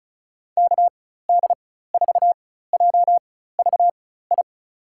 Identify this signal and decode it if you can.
Morse code, heard as a keyed tone, about 36 wpm, spaced as for 16 wpm, 700 Hz. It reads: KD4JVI
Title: USA Call Signs - 6 Characters - 4x Character Spacing - 35wpm